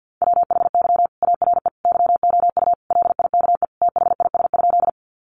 Morse code, heard as a keyed tone, about 48 wpm, 700 Hz. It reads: WHY ARE YOU LIKE THIS?
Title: Sentences from Top 200 Words 50wpm